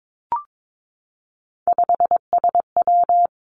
Morse code, heard as a keyed tone, about 22 wpm, 700 Hz. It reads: 5SW
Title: Three Challenging Letter-Number Combinations 22wpm